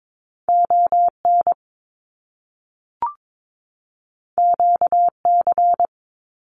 Morse code, heard as a keyed tone, about 22 wpm, 700 Hz. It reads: ON QC
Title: Canadian Provinces and Territories 22wpm